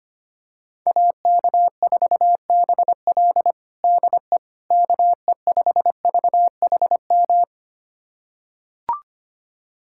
Morse code, heard as a keyed tone, about 25 wpm, 700 Hz. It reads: AK4BL DE KE5VHM
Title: USA Callsign DE Callsign 25wpm